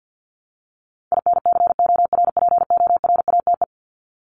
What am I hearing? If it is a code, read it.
Morse code, heard as a keyed tone, about 50 wpm, 700 Hz. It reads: INCORPORATE
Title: Top 1751-2000 Words 50wpm